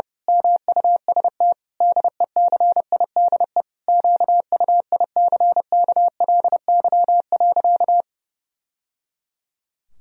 Morse code, heard as a keyed tone, about 30 wpm, 700 Hz. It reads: MUST DECIDE QUICKLY.
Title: Sentences from Top 1000 Words - Review Part 3 - 30wpm